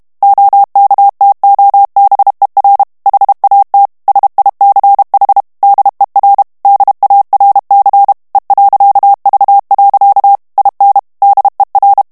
Morse code, heard as a keyed tone, about 32 wpm, 800 Hz. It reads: OKTOBER HAT SICH DER DARC E.V. IN DER
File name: ALLEMAND texte clair 1 vitesse 30 mots minute